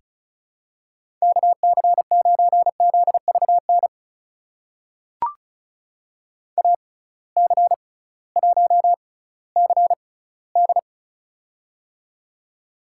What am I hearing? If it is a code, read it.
Morse code, heard as a keyed tone, about 36 wpm, spaced as for 19 wpm, 700 Hz. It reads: KC9ZVN AC1CD
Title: USA Call Signs - 6x Character Spacing - 35wpm